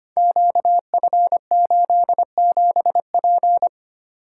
Morse code, heard as a keyed tone, about 25 wpm, 700 Hz. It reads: QF87P
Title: Code Groups - 5 Letter:Numbers - 25wpm